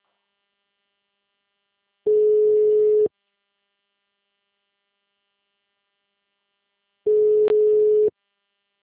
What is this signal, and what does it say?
Ringback tone (Europe, 425 Hz).
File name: out- -2011- - - .19062